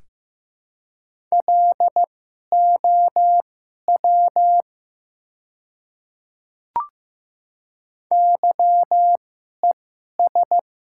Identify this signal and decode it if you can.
Morse code, heard as a keyed tone, about 15 wpm, spaced as for 11 wpm, 700 Hz. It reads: LOW YES
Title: 2x Character Spacing - Top 500 Words - 15wpm